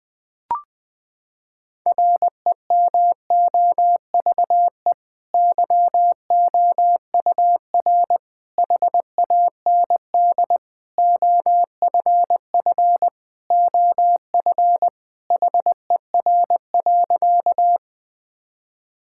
Morse code, heard as a keyed tone, about 20 wpm, 700 Hz. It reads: REMOVE YOUR HAND OFF OF HER.
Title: Sentences from Top 1000 Words - Review Part 5 - 20wpm